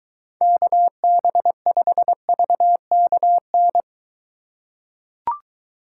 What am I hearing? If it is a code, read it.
Morse code, heard as a keyed tone, about 23 wpm, 700 Hz. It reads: KB5VKN